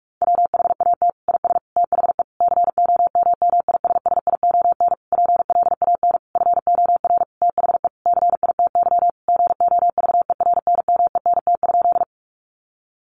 Morse code, heard as a keyed tone, about 44 wpm, 700 Hz. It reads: WHAT IS THE COMMISSION PLAN FOR THE CITY GOVERNMENT?
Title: Sentences from Top 1400 Words 45wpm